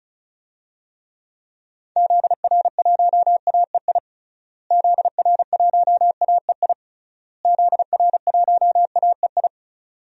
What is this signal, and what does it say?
Morse code, heard as a keyed tone, about 34 wpm, 700 Hz. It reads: ZR1AEI ZR1AEI ZR1AEI
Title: DXCC-100 Call Signs - 6 Characters - Repeated 3x - 35wpm